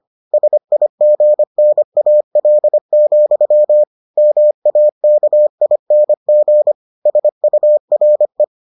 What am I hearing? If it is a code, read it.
Morse code, heard as a keyed tone, about 25 wpm, 600 Hz. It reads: SIGNAL, MAKING SURE